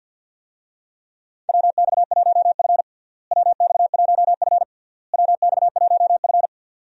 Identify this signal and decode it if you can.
Morse code, heard as a keyed tone, about 51 wpm, 700 Hz. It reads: WX1F WX1F WX1F
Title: USA Call Signs - 4 Characters - Repeated 3x - 50wpm